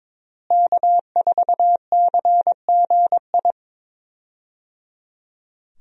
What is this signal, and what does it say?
Morse code, heard as a keyed tone, about 22 wpm, 700 Hz. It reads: K4CGI
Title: USA Call Signs - 5 Characters - Repeated 3x - 22wpm